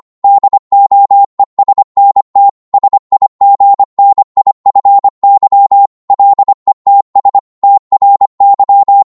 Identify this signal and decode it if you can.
Morse code, heard as a keyed tone, about 25 wpm, 800 Hz. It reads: DOESNT SIGNIFY LETS TRY